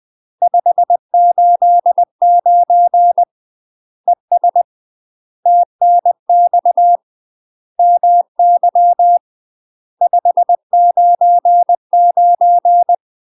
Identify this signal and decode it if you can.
Morse code, heard as a keyed tone, about 20 wpm, 700 Hz. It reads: 589 ES TNX MY 599
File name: UK MASTER QSOs TRAINING / 20WPM / UK MASTER QSOs 0030